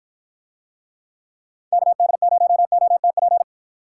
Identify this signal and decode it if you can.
Morse code, heard as a keyed tone, about 48 wpm, 700 Hz. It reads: KD0OTP